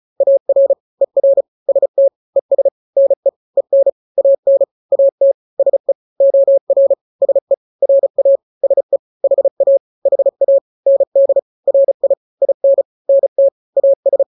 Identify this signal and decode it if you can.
Morse code, heard as a keyed tone, about 34 wpm, 550 Hz. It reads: AR ER ST ES NE EN AN AT SE OR SE RA SE HA HA ND RI IN NT AS
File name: n-grams / 2-grams 35 30 WPM